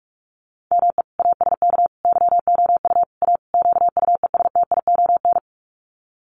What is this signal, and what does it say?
Morse code, heard as a keyed tone, about 44 wpm, 690 Hz. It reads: ME ASK YOU A QUESTION
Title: Sets of 5 Words Encore2 45wpm